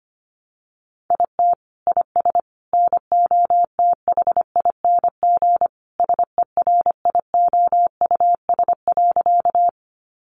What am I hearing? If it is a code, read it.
Morse code, heard as a keyed tone, about 25 wpm, 700 Hz. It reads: IT IS NOTHING SERIOUS.
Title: Sentences from Top 1100 Words 25wpm